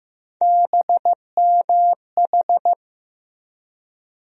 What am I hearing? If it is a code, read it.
Morse code, heard as a keyed tone, about 15 wpm, 700 Hz. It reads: BMH
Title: USA Call Sign Postfixes - 3 Letters 15wpm